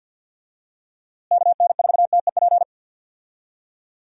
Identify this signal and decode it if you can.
Morse code, heard as a keyed tone, about 50 wpm, 700 Hz. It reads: KN4TEP
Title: USA Call Signs - 6 Characters - Repeated 3x - 50wpm